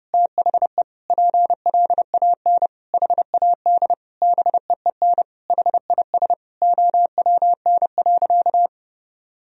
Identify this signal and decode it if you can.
Morse code, heard as a keyed tone, about 32 wpm, 700 Hz. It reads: THE PLAN HAD BEEN HIS OWN.
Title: Sentences from Top 1000 Words - Review Part 2 - 30wpm